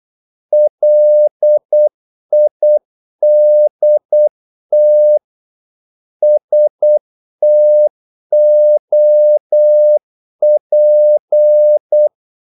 Morse code, heard as a keyed tone, about 8 wpm, 600 Hz. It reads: LIDT STOP
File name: Korte ord DK 8wpm